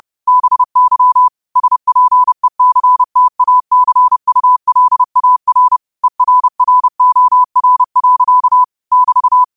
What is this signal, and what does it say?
Morse code, heard as a keyed tone, about 30 wpm, 1000 Hz. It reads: BY SPECTACULAR ERROR. <BT>